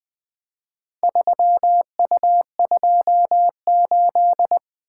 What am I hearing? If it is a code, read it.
Morse code, heard as a keyed tone, about 20 wpm, 700 Hz. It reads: 3U28
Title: Four Letter-Number Combinations - 20wpm